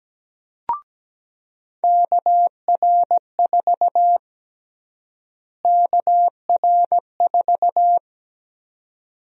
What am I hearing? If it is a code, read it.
Morse code, heard as a keyed tone, about 17 wpm, 700 Hz. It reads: KR4 KR4
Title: USA Call Sign Prefixes - Repeated 2x - 17wpm